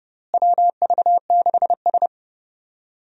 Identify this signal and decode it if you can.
Morse code, heard as a keyed tone, about 30 wpm, 700 Hz. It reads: WV6S
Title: USA Call Signs - 4 Characters - Repeated 3x - 30wpm